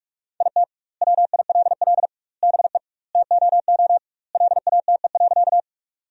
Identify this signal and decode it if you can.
Morse code, heard as a keyed tone, about 46 wpm, 700 Hz. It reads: IT WILL BE TOO LATE.
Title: 2x Word-Spacing - Sentences from Top 300 Words - 45wpm